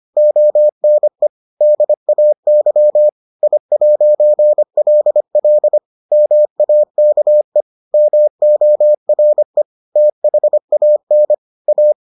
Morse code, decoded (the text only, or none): ONE DAY I'LL MAKE MORE THAN A